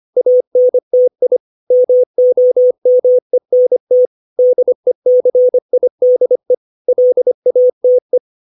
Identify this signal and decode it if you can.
Morse code, heard as a keyed tone, about 25 wpm, 500 Hz. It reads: ANTI MOMENT DECIDE LATE
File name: 632 words / 632 Words 25WPM